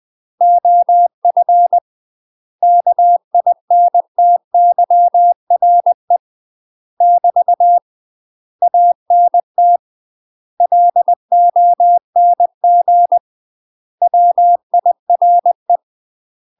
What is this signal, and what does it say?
Morse code, heard as a keyed tone, about 20 wpm, 700 Hz. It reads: OF KINTYRE <BT> ANT LONG WIRE
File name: UK MASTER QSOs TRAINING / 20WPM / UK MASTER QSOs 0030